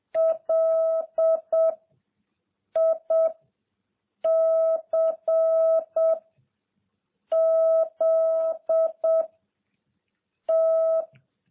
Morse code, heard as a keyed tone, about 7 wpm, 660 Hz. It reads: L I C Z T